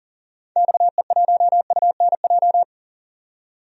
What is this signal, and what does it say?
Morse code, heard as a keyed tone, about 40 wpm, 700 Hz. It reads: XE1UNJ
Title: DXCC-100 Call Signs - 6 Characters - 40wpm